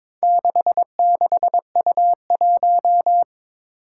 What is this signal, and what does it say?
Morse code, heard as a keyed tone, about 21 wpm, 700 Hz. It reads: 66U1
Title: Four Challenging Letter-Number Combinations 22wpm